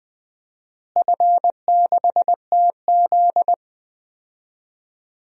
Morse code, heard as a keyed tone, about 21 wpm, 700 Hz. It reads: F6TZ
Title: DXCC-100 Call Signs - 4 Characters - Repeated 3x - 20wpm